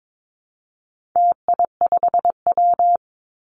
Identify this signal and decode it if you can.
Morse code, heard as a keyed tone, about 22 wpm, 700 Hz. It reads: TI5W
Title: Call Signs - 22wpm